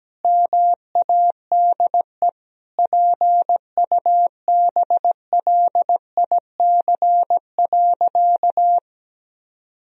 Morse code, heard as a keyed tone, about 17 wpm, 700 Hz. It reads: MADE PUBLIC.
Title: Sentences from Top 1000 Words - Review Part 3 - 17wpm